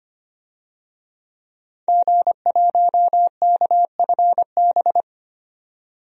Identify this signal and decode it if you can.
Morse code, heard as a keyed tone, about 25 wpm, 700 Hz. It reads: G1KFB
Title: DXCC-100 Call Signs - 5 Characters - 25wpm